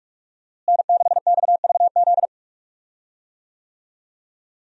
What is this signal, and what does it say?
Morse code, heard as a keyed tone, about 41 wpm, 700 Hz. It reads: N6XVZ
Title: USA Call Signs - 5 Characters - Repeated 3x - 45wpm